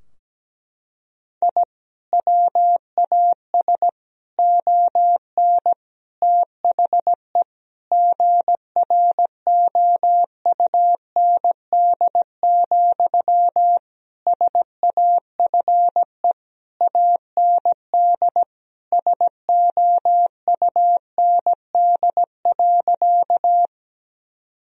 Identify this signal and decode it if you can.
Morse code, heard as a keyed tone, about 17 wpm, 700 Hz. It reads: I WAS ON THE GROUND, SAFE AND SOUND.
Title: Sentences from Top 1000 Words - Review Part 2 - 17wpm